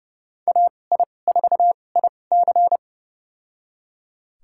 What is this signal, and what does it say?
Morse code, heard as a keyed tone, about 30 wpm, spaced as for 22 wpm, 700 Hz. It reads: AI4IC
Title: USA Call Signs - 5 Characters - 2x Character Spacing - 30wpm